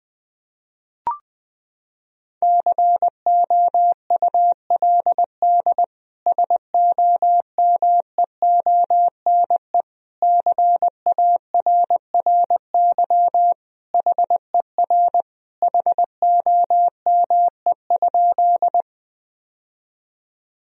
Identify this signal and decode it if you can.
Morse code, heard as a keyed tone, about 21 wpm, 700 Hz. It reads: COULD SOMEONE CARRY HER HOME?
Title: Sentences from Top 1000 Words - Review Part 4 - 20wpm